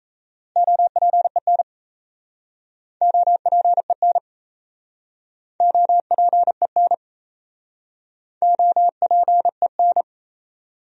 Morse code, heard as a keyed tone, about 38 wpm, 700 Hz. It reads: OPEN OPEN OPEN OPEN